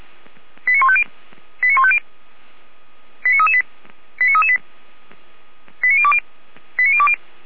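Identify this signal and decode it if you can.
Five-tone selective call (ZVEI), heard as quick runs of five tones, tones of about 70 ms. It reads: 89150 89150 89338 89338 89020 89020
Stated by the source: Five-tone ZVEI selective call received with an SDR.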